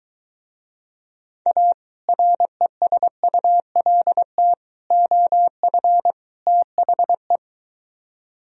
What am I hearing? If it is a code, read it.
Morse code, heard as a keyed tone, about 23 wpm, 700 Hz. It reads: A RESULT OF THE